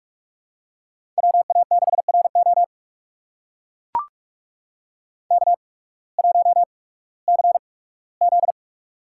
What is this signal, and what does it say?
Morse code, heard as a keyed tone, about 44 wpm, spaced as for 21 wpm, 700 Hz. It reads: WA6RO K1CZ
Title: USA Call Signs - 8x Character Spacing - 45wpm